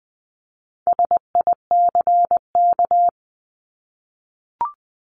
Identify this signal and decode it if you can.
Morse code, heard as a keyed tone, about 20 wpm, 700 Hz. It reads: SICK
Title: 260 Lesson 42e - Words 20-20wpm